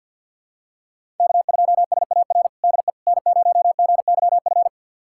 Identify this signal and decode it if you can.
Morse code, heard as a keyed tone, about 47 wpm, 700 Hz. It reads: K2SAR DE N0GYF
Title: USA Callsign DE Callsign - 5 Characters - 50wpm